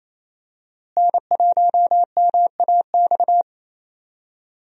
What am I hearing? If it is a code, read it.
Morse code, heard as a keyed tone, about 27 wpm, 700 Hz. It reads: N1MAX
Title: USA Call Signs - 5 Characters - Repeated 3x - 28wpm